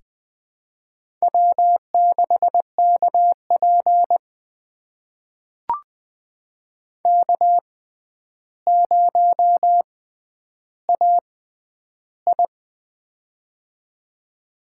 Morse code, heard as a keyed tone, about 20 wpm, spaced as for 10 wpm, 700 Hz. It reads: W6KP K0AI
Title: USA Call Signs - 4 Characters - 6x Character Spacing - 20wpm